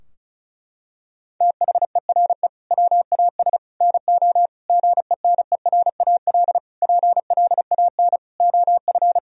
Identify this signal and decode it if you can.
Morse code, heard as a keyed tone, about 34 wpm, 700 Hz. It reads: THERE WAS NO GENERAL PLAN OF
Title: Sentences from Top 1000 Words - Review Part 4 - 35wpm